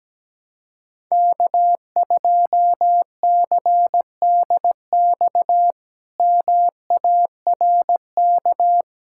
Morse code, heard as a keyed tone, about 17 wpm, 700 Hz. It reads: K2CDX MARK
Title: K1USN SST QSO Practice - 17wpm